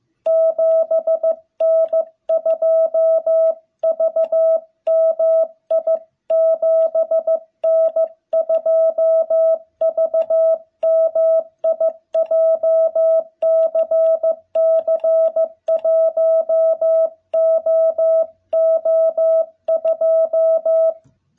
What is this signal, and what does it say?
Morse code, heard as a keyed tone, about 15 wpm, 660 Hz. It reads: 7N2VMI 7N2VMI JCC1OO2
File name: jh8cbh / 2021cw